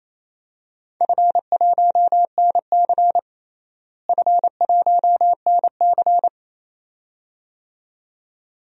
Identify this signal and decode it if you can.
Morse code, heard as a keyed tone, about 28 wpm, 700 Hz. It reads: F1NC F1NC
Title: DXCC-100 Call Signs - 4 Characters - Repeated 2x - 28wpm